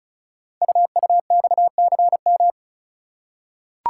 Morse code, heard as a keyed tone, about 36 wpm, 690 Hz. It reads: UUXCM
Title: Code Groups - 5 Letters - 35wpm